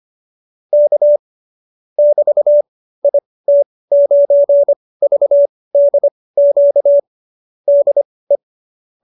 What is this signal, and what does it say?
Morse code, heard as a keyed tone, about 25 wpm, spaced as for 18 wpm, 590 Hz. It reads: K <BT> IT9VDQ DE